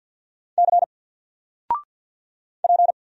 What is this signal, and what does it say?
Morse code, heard as a keyed tone, about 48 wpm, 700 Hz. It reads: C P
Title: Single Character (Letter:Number:Punctuation:Prosign) 50wpm